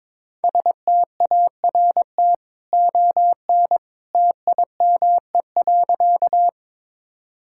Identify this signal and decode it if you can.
Morse code, heard as a keyed tone, about 22 wpm, 700 Hz. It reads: START ON TIME.
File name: practice / 22wpm / Easy Sentences with Hints - 22wpm